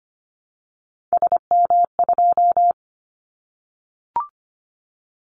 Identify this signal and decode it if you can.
Morse code, heard as a keyed tone, about 25 wpm, 700 Hz. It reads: SM2
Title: Callsign Prefixes for Top 25 DXCC - 25wpm